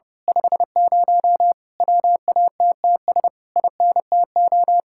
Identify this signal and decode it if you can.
Morse code, heard as a keyed tone, about 28 wpm, 700 Hz. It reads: 50 WATTS INTO
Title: Speed-Racing Common QSO Phrases - 30wpm